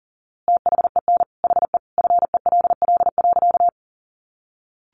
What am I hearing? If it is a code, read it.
Morse code, heard as a keyed tone, about 37 wpm, 700 Hz. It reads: THEN HE FELL.
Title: A Sound Of Thunder 40wpm